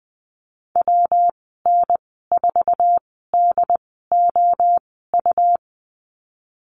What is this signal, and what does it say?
Morse code, heard as a keyed tone, about 20 wpm, spaced as for 15 wpm, 700 Hz. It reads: WN4DOU
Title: 128 Lesson 21g - Number 6 - Call Signs 20wpm